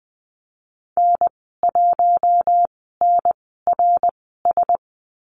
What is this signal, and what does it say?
Morse code, heard as a keyed tone, about 20 wpm, spaced as for 15 wpm, 700 Hz. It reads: N1NRS
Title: 069 Lesson 12g - Letter d - Call Signs 20wpm